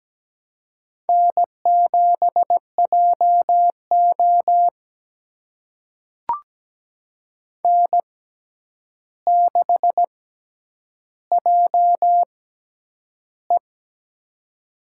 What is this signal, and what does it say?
Morse code, heard as a keyed tone, about 17 wpm, spaced as for 9 wpm, 700 Hz. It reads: N7JO N6JE
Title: USA Call Signs - 4 Characters - 6x Character Spacing - 17wpm